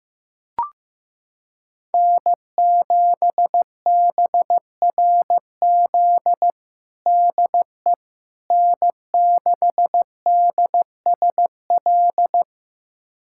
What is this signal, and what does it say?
Morse code, heard as a keyed tone, about 15 wpm, 700 Hz. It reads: N7BRZ DE N6DSL
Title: USA Callsign DE Callsign - 5 Characters - 15wpm